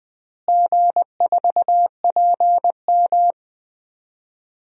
Morse code, heard as a keyed tone, about 20 wpm, 700 Hz. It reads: G4PM
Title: DXCC-100 Call Signs - 4 Characters - Repeated 3x - 20wpm